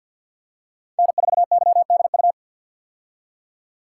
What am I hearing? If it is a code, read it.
Morse code, heard as a keyed tone, about 53 wpm, 700 Hz. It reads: N4YDU
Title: K1USN SST QSO Practice - 50wpm